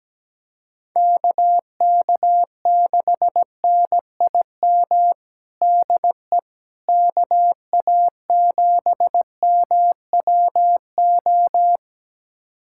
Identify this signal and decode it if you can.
Morse code, heard as a keyed tone, about 17 wpm, 700 Hz. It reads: KK6NIM DE KA7MWO
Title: USA Callsign DE Callsign - 6 Characters - 17wpm